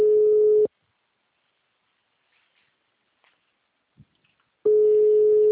Ringback tone (Europe, 425 Hz).